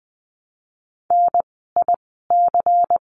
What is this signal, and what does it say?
Morse code, heard as a keyed tone, about 20 wpm, spaced as for 15 wpm, 700 Hz. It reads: NIC
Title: 094 Lesson 16f - Letter c - Words 20wpm